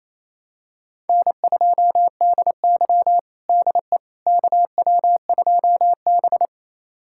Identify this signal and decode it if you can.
Morse code, heard as a keyed tone, about 28 wpm, 700 Hz. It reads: N2DY DE KW2B
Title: USA Callsign DE Callsign - 4 Characters - 28wpm